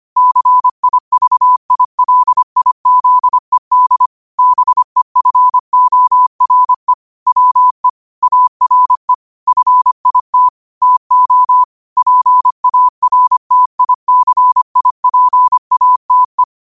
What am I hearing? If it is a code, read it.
Morse code, heard as a keyed tone, about 25 wpm, 1000 Hz. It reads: CIVILIZED BEFORE WE ARE FIT TO PARTICIPATE
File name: 25wpm